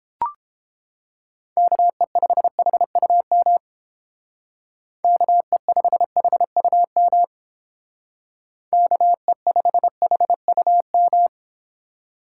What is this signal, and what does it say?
Morse code, heard as a keyed tone, about 30 wpm, 700 Hz. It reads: KE5HUM KE5HUM KE5HUM